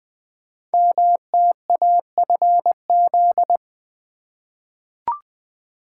Morse code, heard as a keyed tone, about 20 wpm, 700 Hz. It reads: MTAFZ
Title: Code Groups - 5 Letters - 20wpm